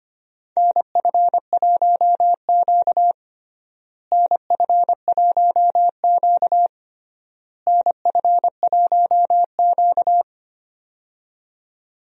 Morse code, heard as a keyed tone, about 25 wpm, 700 Hz. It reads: NF1Q NF1Q NF1Q
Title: USA Call Signs - 4 Characters - Repeated 3x - 25wpm